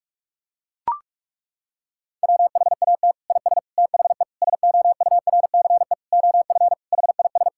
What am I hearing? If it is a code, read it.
Morse code, heard as a keyed tone, about 41 wpm, 700 Hz. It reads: WHAT IS THE SOURCE OF HIS
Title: Sentences from Top 1000 Words - Review Part 5 - 45wpm